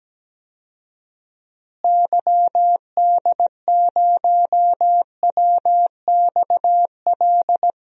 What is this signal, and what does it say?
Morse code, heard as a keyed tone, about 17 wpm, 700 Hz. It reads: YD0WXL
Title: DXCC-100 Call Signs - 6 Characters - Repeated 2x - 17wpm